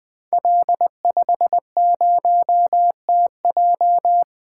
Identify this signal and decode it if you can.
Morse code, heard as a keyed tone, about 20 wpm, 700 Hz. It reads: L50TJ
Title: Code Groups - 5 Letter:Numbers - 20wpm